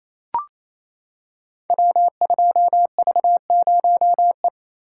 Morse code, heard as a keyed tone, about 27 wpm, 700 Hz. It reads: W2V0E
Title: Code Groups - 5 Letter:Numbers - 28wpm